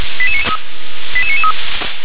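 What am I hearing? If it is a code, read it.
Five-tone selective call (ZVEI), heard as quick runs of five tones, tones of about 70 ms. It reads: 88993
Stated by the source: ZVEI five-tone call 88992 received on an SDR.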